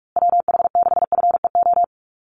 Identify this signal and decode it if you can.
Morse code, heard as a keyed tone, about 44 wpm, 700 Hz. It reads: WH6FEO
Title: USA Call Signs Repeated 3x 45wpm